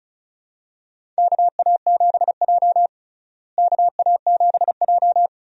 Morse code, heard as a keyed tone, about 34 wpm, 700 Hz. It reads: KA7J KA7J
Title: USA Call Signs - 4 Characters - Repeated 2x - 35wpm